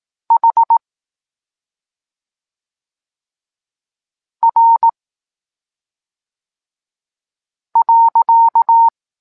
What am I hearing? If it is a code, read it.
Morse code, heard as a keyed tone, about 18 wpm, 920 Hz. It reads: H R .
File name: Tekst-26 1